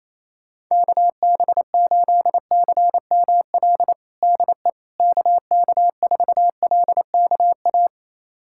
Morse code, heard as a keyed tone, about 28 wpm, 700 Hz. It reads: KB8CML DE KK4LKA
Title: USA Callsign DE Callsign 28wpm